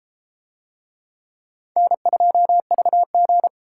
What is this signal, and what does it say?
Morse code, heard as a keyed tone, about 32 wpm, 700 Hz. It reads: N2VG